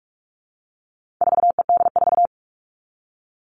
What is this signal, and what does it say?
Morse code, heard as a keyed tone, about 44 wpm, 700 Hz. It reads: 4ED4
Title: All Cut Numbers - 4-Digit 45wpm